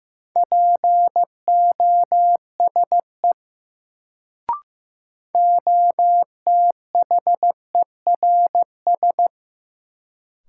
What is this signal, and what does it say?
Morse code, heard as a keyed tone, about 15 wpm, 700 Hz. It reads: POSE OTHERS
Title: Sentences from Top 1000 Words - Review Part 2 - New Word Drill - 15wpm